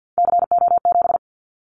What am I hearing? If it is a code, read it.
Morse code, heard as a keyed tone, about 50 wpm, 700 Hz. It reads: CO7
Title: Callsign Prefixes for Top 100 DXCC - 50wpm